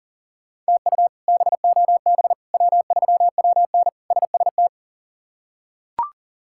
Morse code, heard as a keyed tone, about 40 wpm, 700 Hz. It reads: TU BOB W3WN SST
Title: K1USN SST QSO Practice - 40wpm